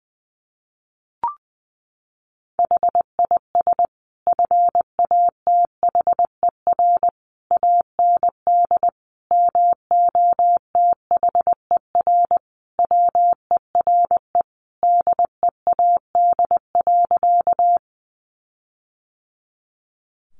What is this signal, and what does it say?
Morse code, heard as a keyed tone, about 20 wpm, 700 Hz. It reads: HIS FATHER AND MOTHER WERE DEAD.
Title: Sentences from Top 1500 Words 20wpm